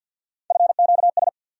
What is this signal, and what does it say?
Morse code, heard as a keyed tone, about 48 wpm, 700 Hz. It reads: FQS
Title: USA Call Sign Postfixes - 3 Letters 50wpm